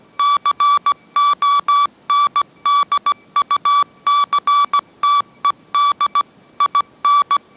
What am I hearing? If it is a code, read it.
Morse code, heard as a keyed tone, about 17 wpm, 1150 Hz. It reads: CONDUCTED IN